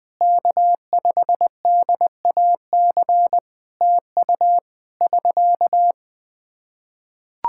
Morse code, heard as a keyed tone, about 20 wpm, 700 Hz. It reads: K5DAC TU <SK>
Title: K1USN SST QSO Practice 20wpm